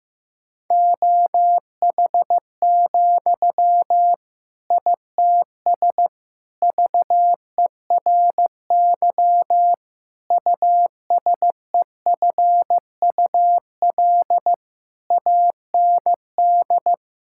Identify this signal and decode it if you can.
Morse code, heard as a keyed tone, about 16 wpm, 700 Hz. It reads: OH, ITS VERY USEFUL AND
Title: Sentences from Top 1000 Words - Review Part 6 - 15wpm